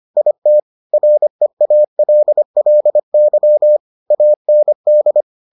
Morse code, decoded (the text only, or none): IT REALLY AND